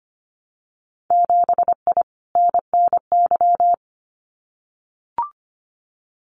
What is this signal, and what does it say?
Morse code, heard as a keyed tone, about 25 wpm, 700 Hz. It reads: 7I NNY
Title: Winter Field Day Exchanges Encore 25wpm